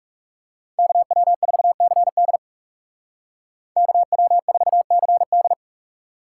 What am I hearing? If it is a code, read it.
Morse code, heard as a keyed tone, about 35 wpm, 700 Hz. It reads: KW4CD KW4CD
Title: Speed Racing - USA 2x2 Call Signs 30wpm